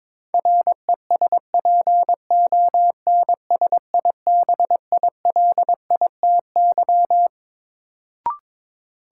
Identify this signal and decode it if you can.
Morse code, heard as a keyed tone, about 22 wpm, 700 Hz. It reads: RESPONSIBILITY
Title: Sentences from Top 1200 Words - New Word Drill - 22wpm